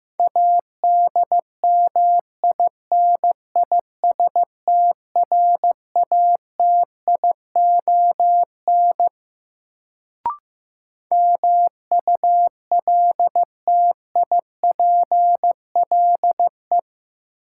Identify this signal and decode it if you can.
Morse code, heard as a keyed tone, about 15 wpm, 700 Hz. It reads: ADMINISTRATION MULTIPLE
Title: Sets of 5 Words - New Word Drill - 15wpm